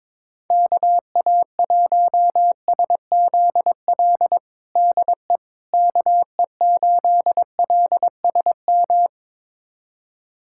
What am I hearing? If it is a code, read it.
Morse code, heard as a keyed tone, about 22 wpm, 700 Hz. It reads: KA1SZL DE KE8LSM
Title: USA Callsign DE Callsign - 6 Characters - 22wpm